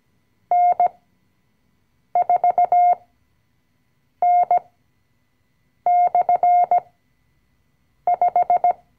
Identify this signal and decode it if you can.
Morse code, heard as a keyed tone, about 17 wpm, spaced as for 6 wpm, 710 Hz. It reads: N4N/5
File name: CW-Online-Kurs / Anfänger / Lektion 020